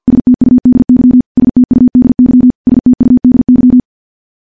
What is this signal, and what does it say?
Morse code, heard as a keyed tone, about 50 wpm, 250 Hz. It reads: STUDY STUDY STUDY
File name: morse-training-50WPM